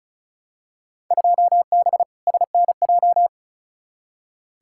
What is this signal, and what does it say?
Morse code, heard as a keyed tone, about 36 wpm, 700 Hz. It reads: 2B SNJ
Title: Field Day Exchanges Encore - 35wpm